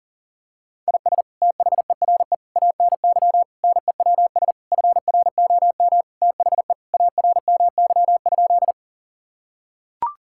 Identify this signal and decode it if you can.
Morse code, heard as a keyed tone, about 40 wpm, 700 Hz. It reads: IS THERE ANY NEWS FROM THE ARMY?
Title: Sentences from Top 1000 Words - Review Part 5 - 40wpm